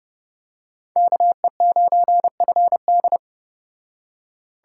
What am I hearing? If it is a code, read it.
Morse code, heard as a keyed tone, about 30 wpm, 700 Hz. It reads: KE9FD